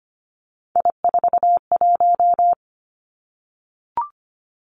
Morse code, heard as a keyed tone, about 25 wpm, 700 Hz. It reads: I41
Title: Three Challenging Letter-Number Combinations 25wpm